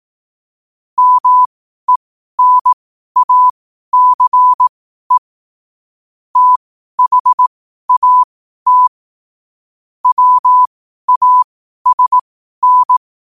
Morse code, heard as a keyed tone, about 18 wpm, spaced as for 12 wpm, 1000 Hz. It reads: MENACE THAT WASN
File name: qotdfiles / 12wpm